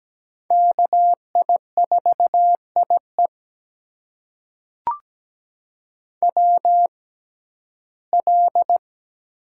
Morse code, heard as a keyed tone, about 17 wpm, spaced as for 12 wpm, 700 Hz. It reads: KI4IE WL
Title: USA Call Signs - 5 Characters - 6x Character Spacing - 17wpm